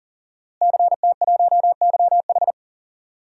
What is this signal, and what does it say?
Morse code, heard as a keyed tone, about 40 wpm, 690 Hz. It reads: CT1YH
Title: DXCC-100 Call Signs - 5 Characters - Repeated 2x - 40wpm